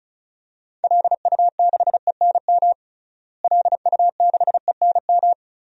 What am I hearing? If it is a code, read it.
Morse code, heard as a keyed tone, about 36 wpm, 700 Hz. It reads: LU6ENM LU6ENM
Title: DXCC-100 Call Signs - 6 Characters - Repeated 3x - 35wpm